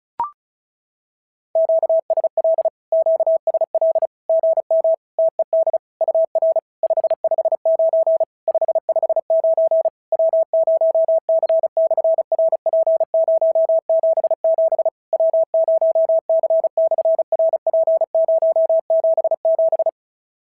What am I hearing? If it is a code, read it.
Morse code, heard as a keyed tone, about 36 wpm, 650 Hz. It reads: QSL QSL GM TED UR 559 559 W0C/RP077 W0C/RP077
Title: SOTA QSO Practice 35wpm